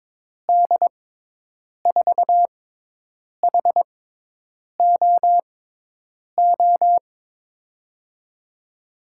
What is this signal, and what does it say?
Morse code, heard as a keyed tone, about 22 wpm, spaced as for 8 wpm, 700 Hz. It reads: D4HOO
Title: USA Call Signs - 6 Characters - 6x Character Spacing - 22wpm